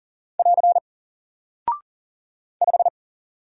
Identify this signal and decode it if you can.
Morse code, heard as a keyed tone, about 40 wpm, 700 Hz. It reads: <AR> 5
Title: Single Character (Letter:Number:Punctuation:Prosign) Alt 40wpm